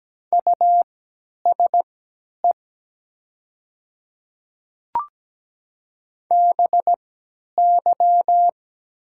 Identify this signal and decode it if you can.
Morse code, heard as a keyed tone, about 17 wpm, spaced as for 10 wpm, 700 Hz. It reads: USE BY
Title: 3x Character Spacing - Top 500 Words - 17wpm